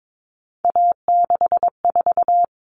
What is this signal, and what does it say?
Morse code, heard as a keyed tone, about 22 wpm, 700 Hz. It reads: A64
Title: All Cut Numbers - 3-Digit 22wpm